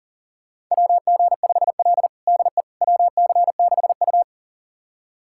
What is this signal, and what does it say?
Morse code, heard as a keyed tone, about 40 wpm, 700 Hz. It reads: WG5L DE WC6U
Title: USA Callsign DE Callsign - 4 Characters - 40wpm